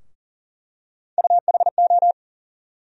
Morse code, heard as a keyed tone, about 40 wpm, 700 Hz. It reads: UHO
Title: USA Call Sign Postfixes - 3 Letters 40wpm